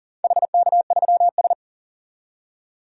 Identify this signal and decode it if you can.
Morse code, heard as a keyed tone, about 40 wpm, 700 Hz. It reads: HK3S
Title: DXCC-100 Call Signs - 4 Characters - 40wpm